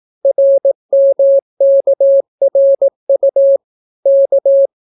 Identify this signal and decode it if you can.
Morse code, heard as a keyed tone, about 18 wpm, 550 Hz. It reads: RMKRU K